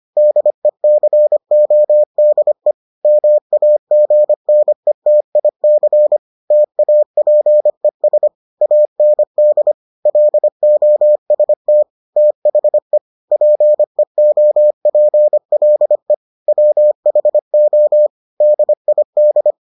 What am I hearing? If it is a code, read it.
Morse code, heard as a keyed tone, about 25 wpm, 600 Hz. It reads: DECODE MAGNETIC TAPES AND LOST THE PEOPLE WHO DID